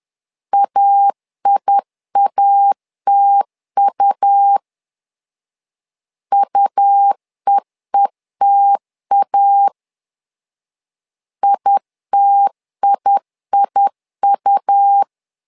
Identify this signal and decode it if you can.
Morse code, heard as a keyed tone, about 10 wpm, 800 Hz. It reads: AIATU UEETA ITIIU
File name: LEZIONI / 2 UHDN / 2 U 10wpm 1622